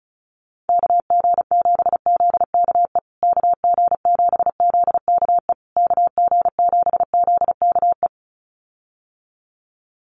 Morse code, heard as a keyed tone, about 34 wpm, 700 Hz. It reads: KG7ZKE KG7ZKE KG7ZKE
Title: USA Call Signs Repeated 3x 35wpm